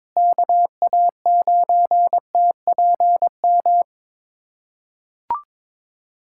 Morse code, heard as a keyed tone, about 22 wpm, 700 Hz. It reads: KA9TPM
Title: USA Callsign DE Callsign - 6 Characters - 22wpm